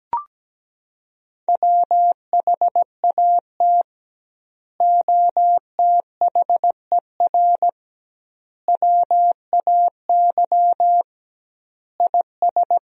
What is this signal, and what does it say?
Morse code, heard as a keyed tone, about 17 wpm, 700 Hz. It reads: WHAT OTHER WAY IS
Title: 2x Word-Spacing - Sentences from Top 100 Words - 17wpm